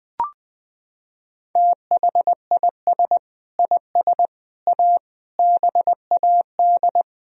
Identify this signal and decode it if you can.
Morse code, heard as a keyed tone, about 20 wpm, 700 Hz. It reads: THIS IS A BAD
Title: Sentences from Top 1000 Words - Review Part 1 - 20wpm